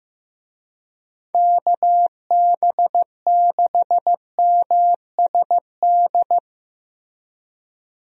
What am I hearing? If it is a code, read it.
Morse code, heard as a keyed tone, about 15 wpm, 700 Hz. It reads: KB6MSD
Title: USA Call Signs - 6 Characters - Repeated 3x - 15wpm